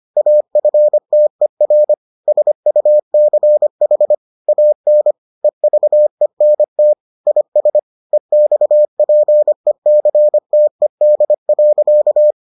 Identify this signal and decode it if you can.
Morse code, heard as a keyed tone, about 25 wpm, 600 Hz. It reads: AFTER SUCH AN EVENT IS EXPECTED.